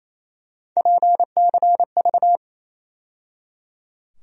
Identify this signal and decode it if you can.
Morse code, heard as a keyed tone, about 28 wpm, 700 Hz. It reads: PCV
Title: USA Call Sign Postfixes - 3 Letters 28wpm